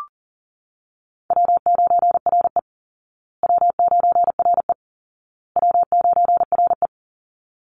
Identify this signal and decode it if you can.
Morse code, heard as a keyed tone, about 37 wpm, 700 Hz. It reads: W9RE W9RE W9RE
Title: Easy Call Signs Repeated 3x - 40wpm